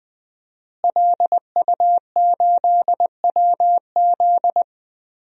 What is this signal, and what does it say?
Morse code, heard as a keyed tone, about 20 wpm, 700 Hz. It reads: LU8WZ
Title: DXCC-100 Call Signs - 5 Characters - Repeated 3x - 20wpm